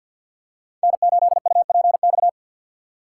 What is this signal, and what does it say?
Morse code, heard as a keyed tone, about 52 wpm, 700 Hz. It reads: N8UPX
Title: USA Call Signs - 6x Character Spacing - 50wpm